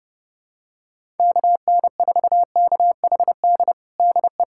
Morse code, heard as a keyed tone, about 30 wpm, 700 Hz. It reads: KN4KHD DE
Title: USA Callsign DE Callsign 30wpm